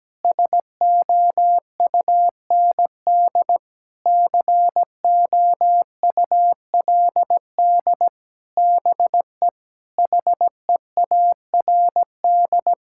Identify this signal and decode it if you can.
Morse code, heard as a keyed tone, about 17 wpm, 700 Hz. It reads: SOUND COULD BE HEARD
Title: Sentences from Top 1000 Words - Review Part 5 - 17wpm